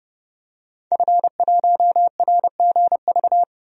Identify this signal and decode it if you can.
Morse code, heard as a keyed tone, about 29 wpm, 700 Hz. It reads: F1RGV
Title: DXCC-100 Call Signs - 5 Characters - 30wpm